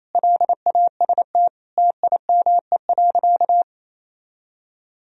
Morse code, heard as a keyed tone, about 27 wpm, 700 Hz. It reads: LAST TIME.
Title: Sentences from Top 1000 Words - Review Part 2 - 28wpm